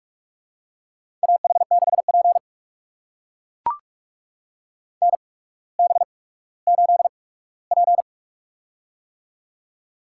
Morse code, heard as a keyed tone, about 44 wpm, spaced as for 19 wpm, 700 Hz. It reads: AH6P NB8P
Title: USA Call Signs - 4 Characters - 8x Character Spacing - 45wpm